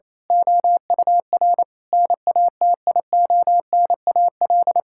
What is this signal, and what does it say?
Morse code, heard as a keyed tone, about 28 wpm, 700 Hz. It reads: OUR NATIONAL
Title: Sentences from Top 1000 Words - Review Part 5 - 28wpm